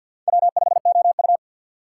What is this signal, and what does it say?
Morse code, heard as a keyed tone, about 52 wpm, 700 Hz. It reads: W5OU
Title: USA Call Signs - 4x Character Spacing - 50wpm